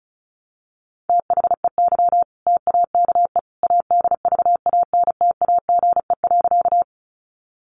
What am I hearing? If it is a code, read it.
Morse code, heard as a keyed tone, about 34 wpm, 700 Hz. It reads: THEY TAKE ADVANTAGE.
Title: Sentences from Top 900 Words 35wpm